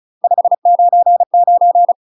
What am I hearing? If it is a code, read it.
Morse code, heard as a keyed tone, about 36 wpm, 700 Hz. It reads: 599
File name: UK MASTER QSOs TRAINING / 35WPM / UK MASTER QSOs 0041